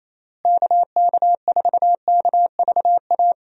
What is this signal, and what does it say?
Morse code, heard as a keyed tone, about 28 wpm, 700 Hz. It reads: KK4KVA
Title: USA Call Signs - 6 Characters - Repeated 2x - 28wpm